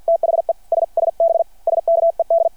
Morse code, heard as a keyed tone, about 51 wpm, 660 Hz. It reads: THE SSB SKED
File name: archived / CRSnet